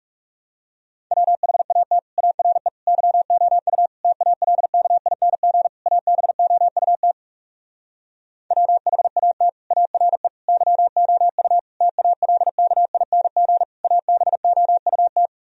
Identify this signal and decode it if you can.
Morse code, heard as a keyed tone, about 40 wpm, 700 Hz. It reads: WHAT ARE YOU TALKING ABOUT WHAT ARE YOU TALKING ABOUT